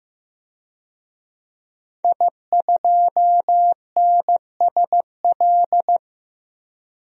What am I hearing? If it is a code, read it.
Morse code, heard as a keyed tone, about 16 wpm, 700 Hz. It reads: I2NSL
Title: DXCC-100 Call Signs - 5 Characters - Repeated 2x - 15wpm